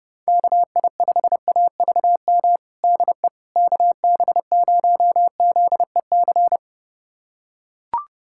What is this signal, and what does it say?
Morse code, heard as a keyed tone, about 30 wpm, 700 Hz. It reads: KI5AVM DE KB0ZEC
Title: USA Callsign DE Callsign - 6 Characters - 30wpm